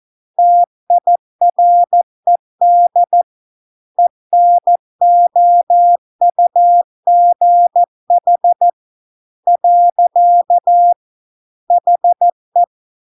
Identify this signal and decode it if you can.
Morse code, heard as a keyed tone, about 14 wpm, 700 Hz. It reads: TIRED ENOUGH . HE